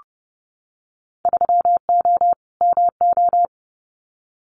Morse code, heard as a keyed tone, about 30 wpm, 700 Hz. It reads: 3O MO
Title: Winter Field Day Exchanges Encore 30wpm